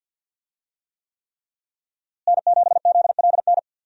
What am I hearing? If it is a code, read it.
Morse code, heard as a keyed tone, about 52 wpm, 700 Hz. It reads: N7ZLN
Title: USA Call Signs - 5 Characters - Repeated 3x - 50wpm